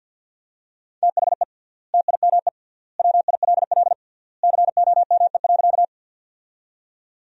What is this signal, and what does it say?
Morse code, heard as a keyed tone, about 51 wpm, 700 Hz. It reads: THE TIME WILL COME.
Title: 3x Word-Spacing - Sentences from Top 100 Words - 50wpm